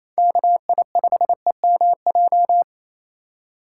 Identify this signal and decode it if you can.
Morse code, heard as a keyed tone, about 28 wpm, 700 Hz. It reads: KI5EMJ
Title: USA Call Signs - 6 Characters - 4x Character Spacing - 28wpm